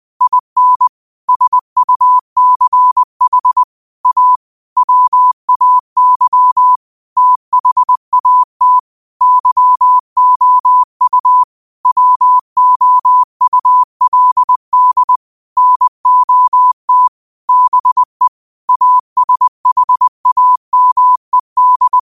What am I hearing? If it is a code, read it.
Morse code, heard as a keyed tone, about 20 wpm, 1000 Hz. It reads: IN SUCH A WAY THAT YOU WOULD NOT BE ASHAMED
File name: 20wpm